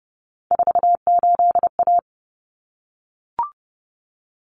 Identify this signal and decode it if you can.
Morse code, heard as a keyed tone, about 30 wpm, 700 Hz. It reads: 48A
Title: Common Cut Numbers - 3-Digit 30wpm